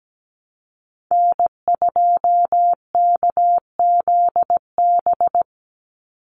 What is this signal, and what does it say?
Morse code, heard as a keyed tone, about 17 wpm, 700 Hz. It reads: N2KZB
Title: USA Call Signs 17wpm